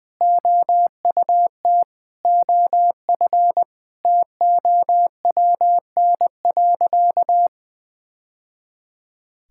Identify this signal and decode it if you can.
Morse code, heard as a keyed tone, about 20 wpm, 700 Hz. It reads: OUT OF TOWN.
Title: Sentences with 3-Word Phrases - 20wpm